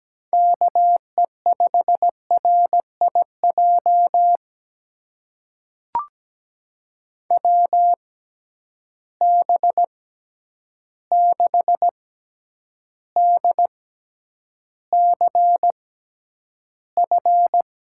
Morse code, heard as a keyed tone, about 17 wpm, spaced as for 9 wpm, 700 Hz. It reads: KE5RIJ WB6DCF
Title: USA Call Signs - 6 Characters - 6x Character Spacing - 17wpm